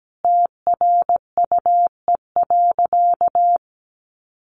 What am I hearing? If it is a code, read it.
Morse code, heard as a keyed tone, about 17 wpm, 700 Hz. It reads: TRUE.
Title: Sentences from Top 500 Words 17wpm